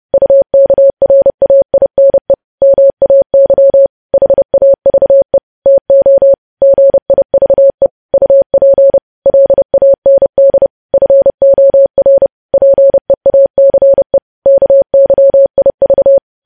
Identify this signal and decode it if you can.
Morse code, heard as a keyed tone, about 30 wpm, 550 Hz. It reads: UKRAINE MAY HAVE TO GIVE UP LAND FOR PEACE KYIV
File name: podcast / News headlines-1x-30x30